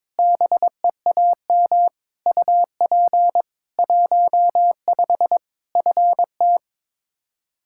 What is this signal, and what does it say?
Morse code, heard as a keyed tone, about 22 wpm, 700 Hz. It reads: BEAM UP 15 FT
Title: K1USN SST QSO Practice 22wpm